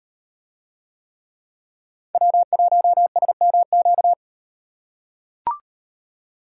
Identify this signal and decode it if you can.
Morse code, heard as a keyed tone, about 37 wpm, 700 Hz. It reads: W1SMQ